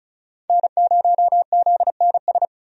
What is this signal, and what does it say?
Morse code, heard as a keyed tone, about 36 wpm, 700 Hz. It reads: N0ZNS
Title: USA Call Signs - 5 Characters - 35wpm